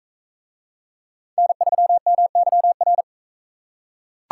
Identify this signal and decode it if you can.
Morse code, heard as a keyed tone, about 42 wpm, 700 Hz. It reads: N3MYR